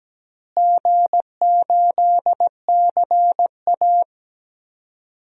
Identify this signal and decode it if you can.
Morse code, heard as a keyed tone, about 17 wpm, 700 Hz. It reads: G8CA
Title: DXCC-100 Call Signs - 4 Characters - Repeated 2x - 17wpm